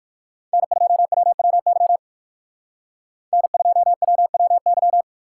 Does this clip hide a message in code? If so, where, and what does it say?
Yes: Morse code from 0.5 to 5.0 s, heard as a keyed tone, about 58 wpm, 700 Hz. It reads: N2WWY N2WWY